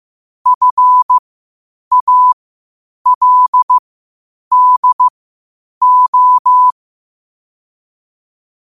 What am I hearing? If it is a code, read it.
Morse code, heard as a keyed tone, about 15 wpm, spaced as for 8 wpm, 1000 Hz. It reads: FALDO
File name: qotdfiles / 8wpm